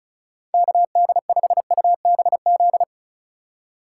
Morse code, heard as a keyed tone, about 34 wpm, 700 Hz. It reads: KD5UBZ
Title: USA Call Signs - 6 Characters - 2x Character Spacing - 35wpm